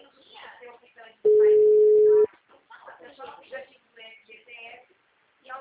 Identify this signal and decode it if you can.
Ringback tone (Europe, 425 Hz), with speech.